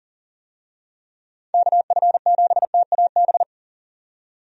Morse code, heard as a keyed tone, about 40 wpm, 700 Hz. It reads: KF7TAB
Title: USA Call Signs - 6 Characters - 40wpm